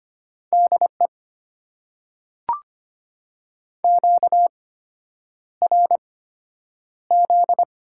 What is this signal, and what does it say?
Morse code, heard as a keyed tone, about 25 wpm, spaced as for 9 wpm, 700 Hz. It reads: DE QRZ
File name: Top 90 QSO Elements - 8x Character Spacing - 25wpm